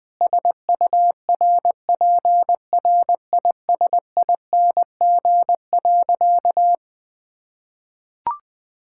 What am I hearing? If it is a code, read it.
Morse code, heard as a keyed tone, about 21 wpm, 700 Hz. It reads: SURPRISING.
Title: Sentences with 4-Word Phrases - Spoken Beginning - 20wpm